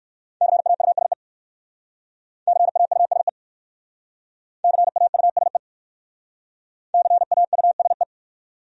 Morse code, heard as a keyed tone, about 59 wpm, 700 Hz. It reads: CAUSE CAUSE CAUSE CAUSE